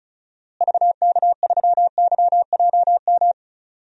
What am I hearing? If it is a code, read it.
Morse code, heard as a keyed tone, about 34 wpm, 700 Hz. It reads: VK3YJM
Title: DXCC-100 Call Signs - 6 Characters - Repeated 2x - 35wpm